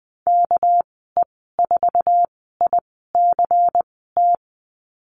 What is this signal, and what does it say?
Morse code, heard as a keyed tone, about 20 wpm, spaced as for 15 wpm, 700 Hz. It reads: KE4ICT
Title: 207 Lesson 33g - Letter j - Call Signs 20wpm